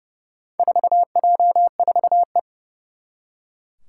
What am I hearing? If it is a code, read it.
Morse code, heard as a keyed tone, about 31 wpm, 700 Hz. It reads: 4J4E
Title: Four Challenging Letter-Number Combinations 30wpm